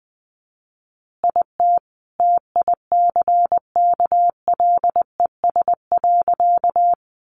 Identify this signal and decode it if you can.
Morse code, heard as a keyed tone, about 20 wpm, 700 Hz. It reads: IT TICKLES.
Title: Additional Easy Phrases 20wpm